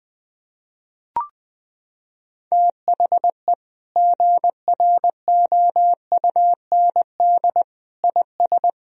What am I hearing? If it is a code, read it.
Morse code, heard as a keyed tone, about 20 wpm, 700 Hz. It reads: THE GROUND IS
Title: Sentences from Top 1000 Words - Review Part 2 - 20wpm